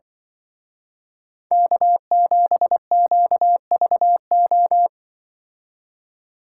Morse code, heard as a keyed tone, about 24 wpm, 700 Hz. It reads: K7QVO
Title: Speed Racing - USA 1x3 Call Signs 20wpm